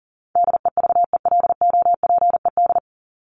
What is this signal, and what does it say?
Morse code, heard as a keyed tone, about 40 wpm, 700 Hz. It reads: DEVELOPED
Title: ED Suffix - Top 100 Words with Base Words 40wpm